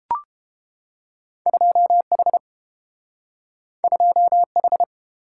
Morse code, heard as a keyed tone, about 30 wpm, 700 Hz. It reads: 2H 2H